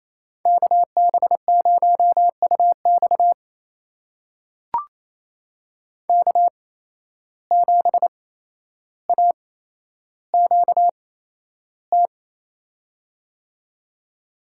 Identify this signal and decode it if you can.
Morse code, heard as a keyed tone, about 28 wpm, spaced as for 12 wpm, 700 Hz. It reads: KB0UX K7AQT
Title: USA Call Signs - 5 Characters - 8x Character Spacing - 28wpm